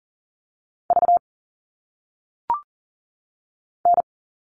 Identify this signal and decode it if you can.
Morse code, heard as a keyed tone, about 38 wpm, 700 Hz. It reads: V N
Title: All Cut Numbers - 1-Digit 40wpm